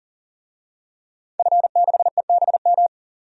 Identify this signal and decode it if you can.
Morse code, heard as a keyed tone, about 40 wpm, 700 Hz. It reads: F6EBM
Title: DXCC-100 Call Signs - 5 Characters - 40wpm